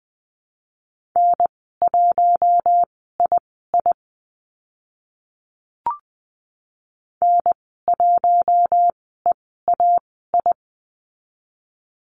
Morse code, heard as a keyed tone, about 20 wpm, spaced as for 14 wpm, 700 Hz. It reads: N1II N1EAI
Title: 042 Lesson 8g - Number 1 - Call Signs 20wpm